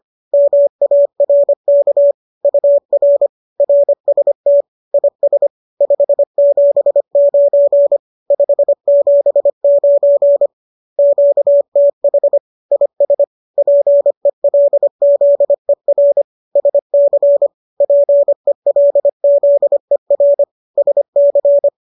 Morse code, heard as a keyed tone, about 25 wpm, 570 Hz. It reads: MARK UR RST IS 579 579 QTH IS PELZER SC PELZER SC
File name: qso206 25